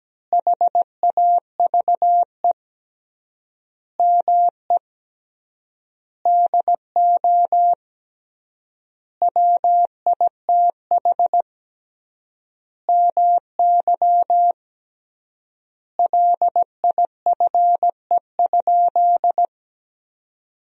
Morse code, heard as a keyed tone, about 17 wpm, 700 Hz. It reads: HAVE ME DO WITH MY LIFE?
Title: 3x Word-Spacing - Sentences from Top 300 Words - 17wpm